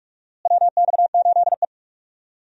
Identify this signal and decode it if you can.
Morse code, heard as a keyed tone, about 44 wpm, 700 Hz. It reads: WX8E
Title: USA Call Signs - 4 Characters - Repeated 2x - 45wpm